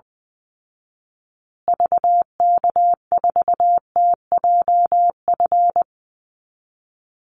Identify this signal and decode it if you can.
Morse code, heard as a keyed tone, about 20 wpm, 700 Hz. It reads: VK4TJF
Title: Call Signs Repeated 3x - 20wpm